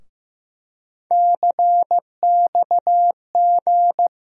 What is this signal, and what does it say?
Morse code, heard as a keyed tone, about 16 wpm, 700 Hz. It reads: CXG
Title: USA Call Sign Postfixes - 3 Letters 15wpm